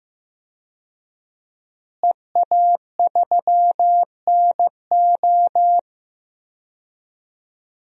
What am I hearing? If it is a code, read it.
Morse code, heard as a keyed tone, about 15 wpm, 700 Hz. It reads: EA3NO
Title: DXCC-100 Call Signs - 5 Characters - Repeated 2x - 15wpm